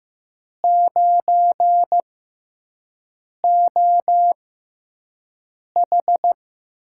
Morse code, heard as a keyed tone, about 15 wpm, spaced as for 5 wpm, 700 Hz. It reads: 9OH
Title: USA Call Signs - 6x Character Spacing - 15wpm